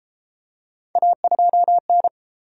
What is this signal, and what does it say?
Morse code, heard as a keyed tone, about 31 wpm, 700 Hz. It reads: A2N